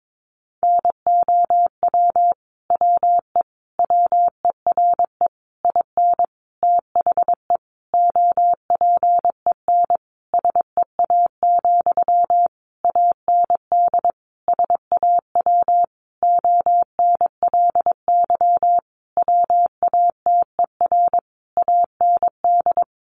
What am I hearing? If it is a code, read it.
Morse code, heard as a keyed tone, about 22 wpm, 700 Hz. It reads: NOW WE WERE IN THE OPEN SEA, AND SAW ONLY WATER AND
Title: Sentences from Top 1200 Words 22wpm